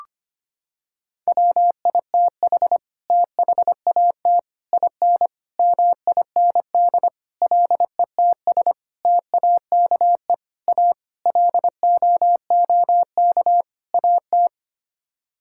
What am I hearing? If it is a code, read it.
Morse code, heard as a keyed tone, about 25 wpm, 700 Hz. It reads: WITH THAT IN MIND LETS TAKE A LOOK AT
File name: Sets of 9 Words - 25wpm